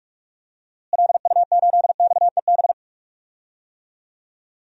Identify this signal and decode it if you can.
Morse code, heard as a keyed tone, about 43 wpm, 700 Hz. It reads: LU8XEB
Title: DXCC-100 Call Signs - 6 Characters - Repeated 2x - 45wpm